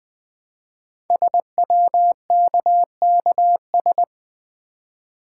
Morse code, heard as a keyed tone, about 20 wpm, 700 Hz. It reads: SWKKS
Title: Code Groups - 5 Letters - 20wpm